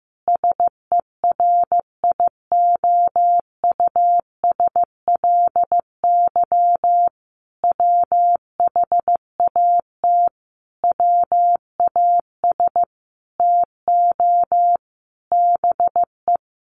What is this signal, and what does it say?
Morse code, heard as a keyed tone, about 15 wpm, 700 Hz. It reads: SERIOUSLY WHAT WAS TO BE
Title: Sentences from Top 1200 Words 15wpm